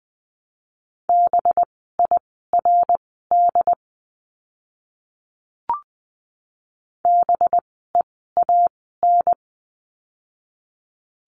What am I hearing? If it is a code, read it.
Morse code, heard as a keyed tone, about 20 wpm, spaced as for 15 wpm, 700 Hz. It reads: BIRD BEAN
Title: 185 Lesson 30f - Letter b - Words 20wpm